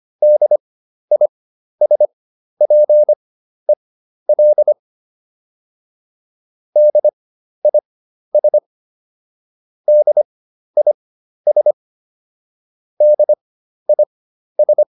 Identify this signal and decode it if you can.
Morse code, heard as a keyed tone, about 25 wpm, spaced as for 12 wpm, 600 Hz. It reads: DISPEL DIS DIS DIS